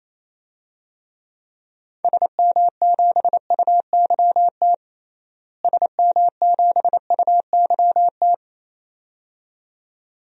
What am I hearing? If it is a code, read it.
Morse code, heard as a keyed tone, about 28 wpm, 700 Hz. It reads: SM7UYT SM7UYT
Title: DXCC-100 Call Signs - 6 Characters - Repeated 2x - 28wpm